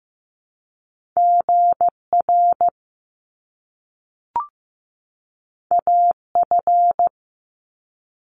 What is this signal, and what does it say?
Morse code, heard as a keyed tone, about 15 wpm, 700 Hz. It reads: GR AF
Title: Top Two Letter Combinations in Words 15wpm